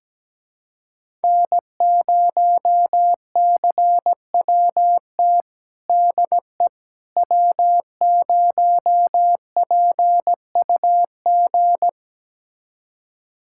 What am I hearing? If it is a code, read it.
Morse code, heard as a keyed tone, about 17 wpm, 700 Hz. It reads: N0CWT DE W0PUG
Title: USA Callsign DE Callsign - 5 Characters - 17wpm